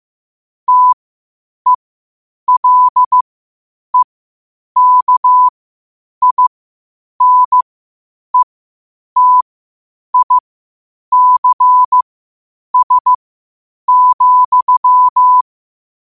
Morse code, heard as a keyed tone, about 15 wpm, spaced as for 8 wpm, 1000 Hz. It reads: TELEKINETICS,
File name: qotdfiles / 8wpm